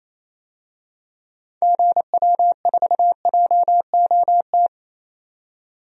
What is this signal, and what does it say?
Morse code, heard as a keyed tone, about 28 wpm, 700 Hz. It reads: GW4JOT
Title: DXCC-100 Call Signs - 6 Characters - 28wpm